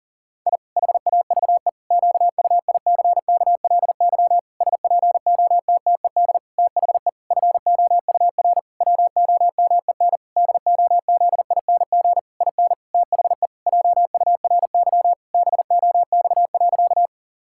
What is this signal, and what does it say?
Morse code, heard as a keyed tone, about 40 wpm, 700 Hz. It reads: I HAVE QUICKLY SPOTTED THE FOUR WOMEN DOZING IN THE JURY BOX.
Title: Pangram Challenge v2 - 40wpm